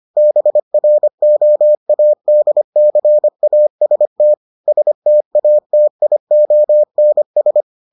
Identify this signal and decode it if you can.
Morse code, heard as a keyed tone, about 25 wpm, 600 Hz. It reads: BROADCAST STATIONS